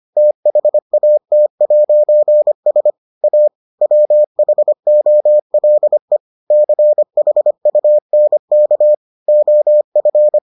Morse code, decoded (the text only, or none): THAT'S A WHOLE CHUNK OF